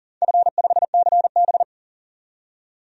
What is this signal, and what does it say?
Morse code, heard as a keyed tone, about 40 wpm, 690 Hz. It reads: F5CB
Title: DXCC-100 Call Signs - 4 Characters - 40wpm